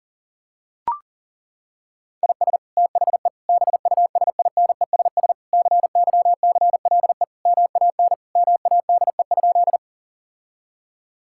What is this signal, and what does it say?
Morse code, heard as a keyed tone, about 37 wpm, 700 Hz. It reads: IS THE BUSINESS CYCLE MAN MADE?
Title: Sentences from Top 1000 Words - Review Part 6 - 40wpm